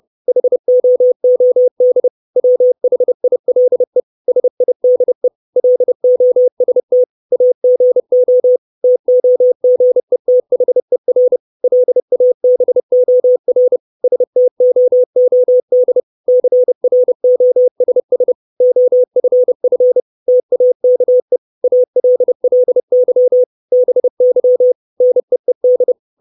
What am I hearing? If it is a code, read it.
Morse code, heard as a keyed tone, about 30 wpm, 500 Hz. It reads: HOOD WHILE SIDE LOST AGO TOGETHER LABOR STOOD CROSS OFF TAKE ALLY BY NEED
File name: 632 Words 30WPM